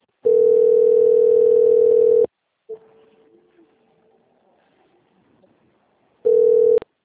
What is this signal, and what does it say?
Ringback tone (North America), 440 + 480 Hz.